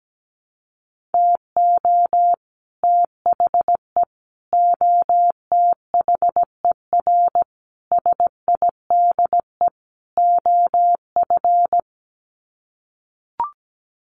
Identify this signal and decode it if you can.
Morse code, heard as a keyed tone, about 17 wpm, 700 Hz. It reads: TO THE OTHER SIDE OF
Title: Sets of 5 Words Encore 17wpm